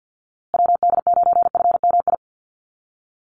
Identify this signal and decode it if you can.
Morse code, heard as a keyed tone, about 50 wpm, 700 Hz. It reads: WD9FMI
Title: USA Call Signs Repeated 3x 50wpm